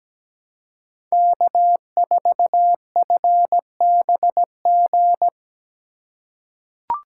Morse code, heard as a keyed tone, about 17 wpm, 700 Hz. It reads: K4FBG
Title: USA Call Signs - 5 Characters - 3x Character Spacing - 17wpm